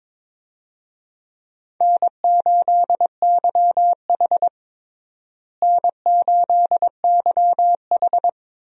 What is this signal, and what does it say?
Morse code, heard as a keyed tone, about 22 wpm, 700 Hz. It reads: N8YH N8YH
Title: USA Call Signs - 4 Characters - Repeated 2x - 22wpm